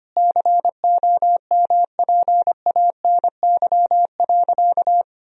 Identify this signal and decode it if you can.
Morse code, heard as a keyed tone, about 25 wpm, 700 Hz. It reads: COMPANY.
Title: Sentences from Top 1000 Words - Review Part 3 - 25wpm